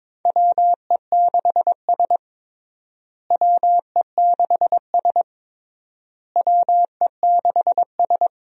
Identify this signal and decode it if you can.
Morse code, heard as a keyed tone, about 22 wpm, 700 Hz. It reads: WE6S WE6S WE6S
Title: USA Call Signs - 4 Characters - Repeated 3x - 22wpm